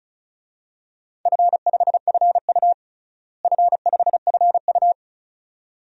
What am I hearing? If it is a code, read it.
Morse code, heard as a keyed tone, about 36 wpm, 700 Hz. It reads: F5FU F5FU
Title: DXCC-100 Call Signs - 4 Characters - Repeated 2x - 35wpm